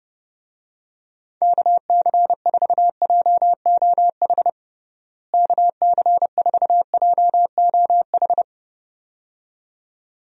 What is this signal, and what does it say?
Morse code, heard as a keyed tone, about 32 wpm, 700 Hz. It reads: KC4JOH KC4JOH
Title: USA Call Signs - 6 Characters - Repeated 2x - 30wpm